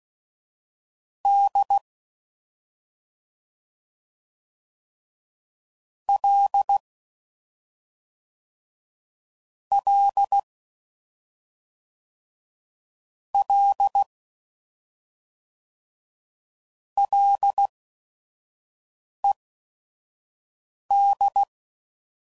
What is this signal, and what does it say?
Morse code, heard as a keyed tone, about 16 wpm, spaced as for 5 wpm, 800 Hz. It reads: D L L L LED
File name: Prefixes / FLs & THs / Single Fs Ls